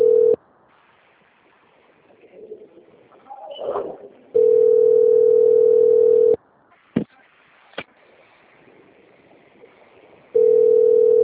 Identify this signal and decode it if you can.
Ringback tone (North America), 440 + 480 Hz.
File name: out- - - - - .72033